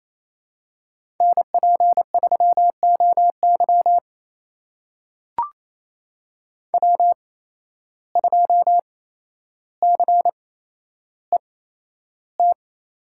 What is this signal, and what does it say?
Morse code, heard as a keyed tone, about 28 wpm, spaced as for 12 wpm, 700 Hz. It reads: NP3OY W2CET
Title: USA Call Signs - 8x Character Spacing - 28wpm